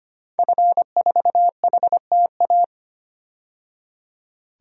Morse code, heard as a keyed tone, about 25 wpm, 700 Hz. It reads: F4HTA
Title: DXCC-100 Call Signs - 5 Characters - 25wpm